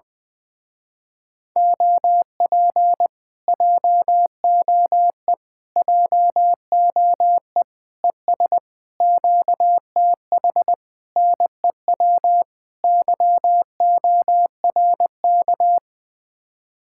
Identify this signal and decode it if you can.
Morse code, heard as a keyed tone, about 20 wpm, 700 Hz. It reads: OP JOE JOE ES QTH NEW YORK
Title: Speed-Racing Common QSO Phrases - 17wpm